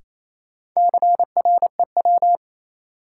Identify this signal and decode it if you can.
Morse code, heard as a keyed tone, about 28 wpm, 700 Hz. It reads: CREW
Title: Sentences from Top 1500 Words - New Word Drill - 28wpm